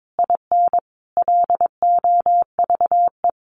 Morse code, heard as a keyed tone, about 22 wpm, 700 Hz. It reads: IN LOVE
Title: Sentences from Top 1300 Words 22wpm